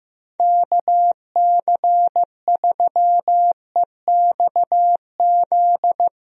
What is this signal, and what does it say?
Morse code, heard as a keyed tone, about 15 wpm, 700 Hz. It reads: KC3EXZ
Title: USA Call Signs - 6 Characters - Repeated 3x - 15wpm